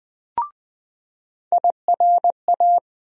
Morse code, heard as a keyed tone, about 20 wpm, 700 Hz. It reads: IRA
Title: USA Call Sign Postfixes - 3 Letters 20wpm